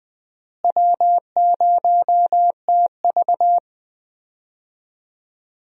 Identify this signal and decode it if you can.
Morse code, heard as a keyed tone, about 20 wpm, 700 Hz. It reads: W0TV
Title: USA Call Signs - 4 Characters - Repeated 3x - 20wpm